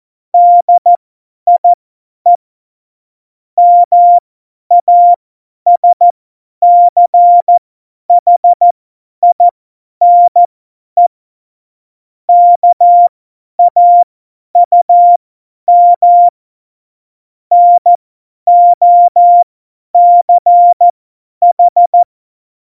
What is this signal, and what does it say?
Morse code, heard as a keyed tone, about 14 wpm, spaced as for 10 wpm, 700 Hz. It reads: DIE MASCHINE KAUM NOCH
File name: Morsekurs / Witze / Witz 22. Dezember 2020, 50 F70